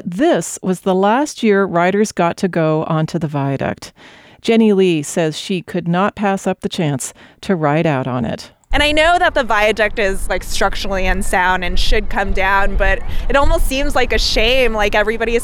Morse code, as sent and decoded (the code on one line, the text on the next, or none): none